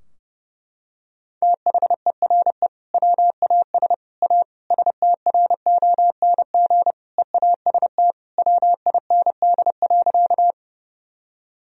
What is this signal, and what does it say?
Morse code, heard as a keyed tone, about 30 wpm, 700 Hz. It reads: THERE WAS A STRONG EAST WIND.
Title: Sentences from Top 1000 Words - Review Part 3 - 30wpm